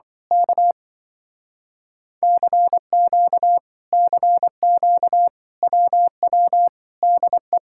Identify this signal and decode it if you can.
Morse code, heard as a keyed tone, about 25 wpm, 700 Hz. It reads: K CQ CQ WW DE
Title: Speed-Racing Common QSO Phrases - 20wpm